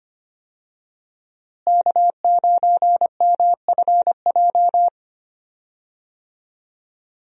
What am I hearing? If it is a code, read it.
Morse code, heard as a keyed tone, about 25 wpm, 700 Hz. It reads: K9MFJ
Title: USA Call Signs - 5 Characters - Repeated 3x - 25wpm